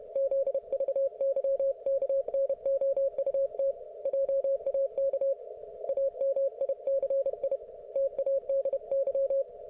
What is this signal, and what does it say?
Morse code, heard as a keyed tone, about 31 wpm, 530 Hz. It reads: ZVYKNOUT JAK AMICI TADY